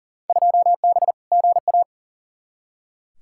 Morse code, heard as a keyed tone, about 40 wpm, 700 Hz. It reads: 2B GA
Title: Field Day Exchanges Encore - 40wpm